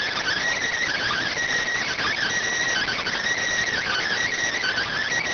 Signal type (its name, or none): Emergency Alert System (SAME) tones